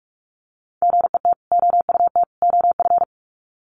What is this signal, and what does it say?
Morse code, heard as a keyed tone, about 44 wpm, 700 Hz. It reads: GET OUT OF
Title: Sets of 3 Words 45wpm